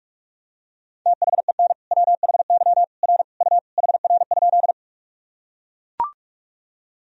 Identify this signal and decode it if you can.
Morse code, heard as a keyed tone, about 42 wpm, 700 Hz. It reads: THEN WHY R U HR?
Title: Sentences from Top 200 Words and Abbreviations 45wpm